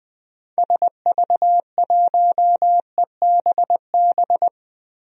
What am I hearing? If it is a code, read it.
Morse code, heard as a keyed tone, about 20 wpm, 700 Hz. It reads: SV1EBB
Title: DXCC-100 Call Signs - 6 Characters - 20wpm